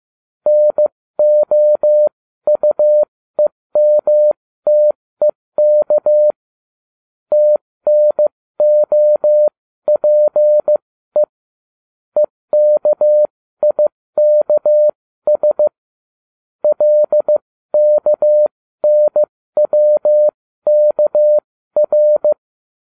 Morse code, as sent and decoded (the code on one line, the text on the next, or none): -. --- ..- . -- - . -.- / - -. --- .--. / . -.- .. -.- ... / .-.. -.- -. .-- -.- .-.
NOUEMTEK TNOPE EKIKS LKNWKR